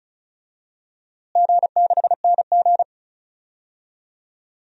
Morse code, heard as a keyed tone, about 36 wpm, 700 Hz. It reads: G6NG
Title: DXCC-100 Call Signs - 4 Characters - Repeated 2x - 35wpm